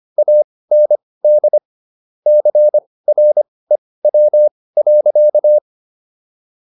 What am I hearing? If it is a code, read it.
Morse code, heard as a keyed tone, about 25 wpm, spaced as for 18 wpm, 600 Hz. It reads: AND CREW.